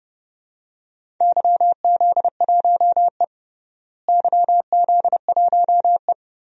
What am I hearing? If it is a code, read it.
Morse code, heard as a keyed tone, about 30 wpm, 700 Hz. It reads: YZ1E YZ1E
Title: DXCC-100 Call Signs - 4 Characters - Repeated 2x - 30wpm